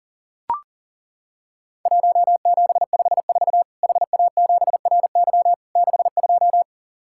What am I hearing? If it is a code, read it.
Morse code, heard as a keyed tone, about 40 wpm, 700 Hz. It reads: 1754 HA7RY 62
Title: Worked All Europe DX Contest - QTC Exchange - 40wpm